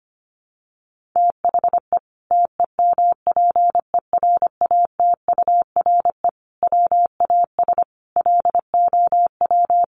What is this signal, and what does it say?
Morse code, heard as a keyed tone, about 25 wpm, 700 Hz. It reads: THE TEMPERATURE WAS LOW
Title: Sentences from Top 1500 Words 25wpm